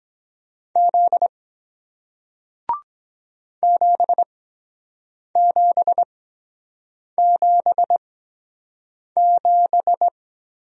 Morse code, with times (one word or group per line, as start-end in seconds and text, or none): 0.8-1.3 Z
3.6-10.1 7777